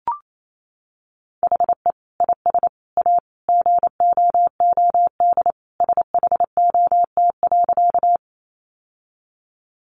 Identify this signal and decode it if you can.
Morse code, heard as a keyed tone, about 28 wpm, 700 Hz. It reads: HE IS A GOOD SHOT.
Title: Sentences from Top 1100 Words 28wpm